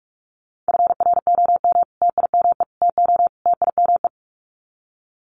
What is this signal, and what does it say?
Morse code, heard as a keyed tone, about 45 wpm, 700 Hz. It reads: FROM TIME TO TIME
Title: Sets of 4 Words 45wpm